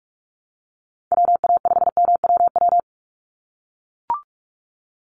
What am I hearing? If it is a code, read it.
Morse code, heard as a keyed tone, about 44 wpm, 700 Hz. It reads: WA5MWW
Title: USA Call Signs 45wpm